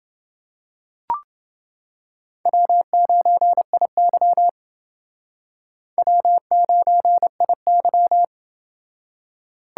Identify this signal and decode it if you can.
Morse code, heard as a keyed tone, about 28 wpm, 700 Hz. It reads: W9IY W9IY